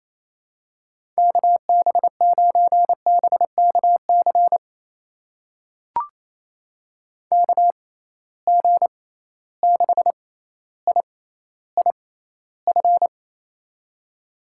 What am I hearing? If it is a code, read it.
Morse code, heard as a keyed tone, about 28 wpm, spaced as for 14 wpm, 700 Hz. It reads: KB9BKC KG6IIF
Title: USA Call Signs - 6 Characters - 6x Character Spacing - 28wpm